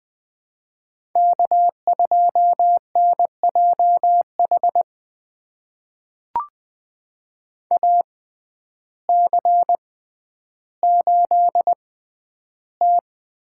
Morse code, heard as a keyed tone, about 20 wpm, spaced as for 11 wpm, 700 Hz. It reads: K2NJH AC8T
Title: USA Call Signs - 5 Characters - 6x Character Spacing - 20wpm